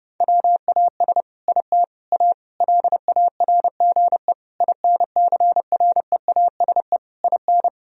Morse code, heard as a keyed tone, about 32 wpm, 700 Hz. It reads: WAS IT A LARGE INCREASE IN
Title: Sentences from Top 1000 Words - Review Part 4 - 30wpm